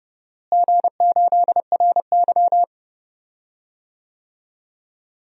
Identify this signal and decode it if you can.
Morse code, heard as a keyed tone, about 30 wpm, 700 Hz. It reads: G8RY
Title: DXCC-100 Call Signs - 4 Characters - Repeated 2x - 30wpm